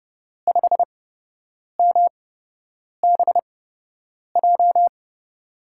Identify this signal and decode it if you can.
Morse code, heard as a keyed tone, about 29 wpm, spaced as for 8 wpm, 700 Hz. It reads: 5MBJ
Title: USA Call Signs - 5 Characters - 8x Character Spacing - 30wpm